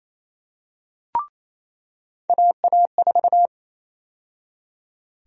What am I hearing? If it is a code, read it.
Morse code, heard as a keyed tone, about 28 wpm, 700 Hz. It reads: AA4
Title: USA Call Sign Prefixes - 28wpm